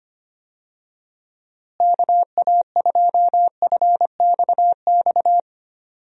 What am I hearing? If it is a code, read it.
Morse code, heard as a keyed tone, about 25 wpm, 700 Hz. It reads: KA2FXX
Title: USA Call Signs - 6 Characters - Repeated 3x - 25wpm